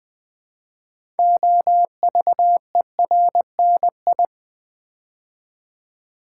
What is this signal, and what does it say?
Morse code, heard as a keyed tone, about 20 wpm, 700 Hz. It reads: OVERNI
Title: Build A Word - 3 Letters At A Time - 9-Letter Words - 20wpm